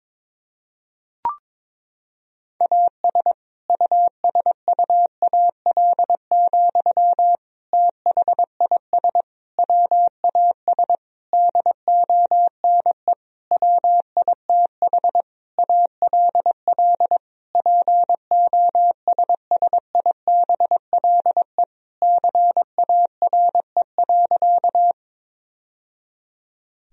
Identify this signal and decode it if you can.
Morse code, heard as a keyed tone, about 22 wpm, 700 Hz. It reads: AS USUAL, THIS WAS DONE WITH ALL POSSIBLE CARE.
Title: Sentences from Top 1000 Words - Review Part 2 - 22wpm